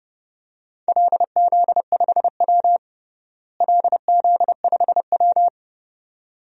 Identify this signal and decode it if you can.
Morse code, heard as a keyed tone, about 30 wpm, 700 Hz. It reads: LZ5W LZ5W
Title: DXCC-100 Call Signs - 4 Characters - Repeated 2x - 30wpm